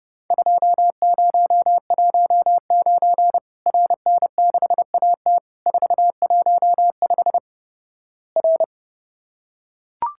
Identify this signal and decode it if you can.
Morse code, heard as a keyed tone, about 29 wpm, 700 Hz. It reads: 2019 RN6AT 415 R
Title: Worked All Europe DX Contest - QTC Exchange - 30wpm